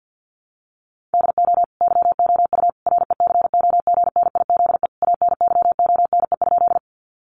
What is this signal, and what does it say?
Morse code, heard as a keyed tone, about 50 wpm, 700 Hz. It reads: DO YOU RECOGNIZE ANYONE?
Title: Sentences from Top 1400 Words 50wpm